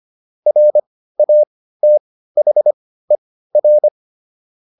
Morse code, heard as a keyed tone, about 25 wpm, spaced as for 15 wpm, 600 Hz. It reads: RATHER
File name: SS215 15